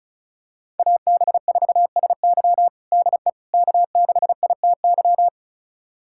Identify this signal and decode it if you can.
Morse code, heard as a keyed tone, about 36 wpm, 700 Hz. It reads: AB4SY DE K6ITY
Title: USA Callsign DE Callsign - 5 Characters - 35wpm